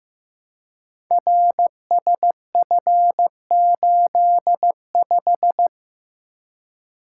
Morse code, heard as a keyed tone, about 15 wpm, 700 Hz. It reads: RSF85
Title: Code Groups - 5 Letter:Numbers - 15wpm